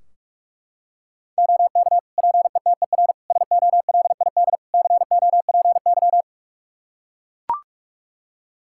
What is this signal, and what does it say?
Morse code, heard as a keyed tone, about 42 wpm, 700 Hz. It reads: OK PETER SOLID COPY
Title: K1USN SST QSO Practice 45wpm